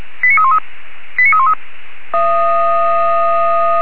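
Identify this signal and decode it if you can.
Five-tone selective call (ZVEI), heard as quick runs of five tones, tones of about 70 ms. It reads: 89313 89313
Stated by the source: ZVEI five-tone selective call received on an SDR by fire brigade Orlamünde (feuerwehr-orlamuende).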